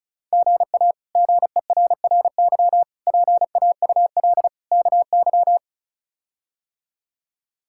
Morse code, heard as a keyed tone, about 34 wpm, 700 Hz. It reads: GA GERRY PAUL KY
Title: K1USN SST QSO Practice - 35wpm